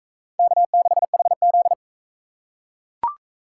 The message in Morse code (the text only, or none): K6HZ